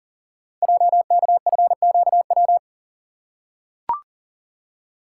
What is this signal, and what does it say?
Morse code, heard as a keyed tone, about 41 wpm, 690 Hz. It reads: JKFQW
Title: Code Groups - 5 Letters - 40wpm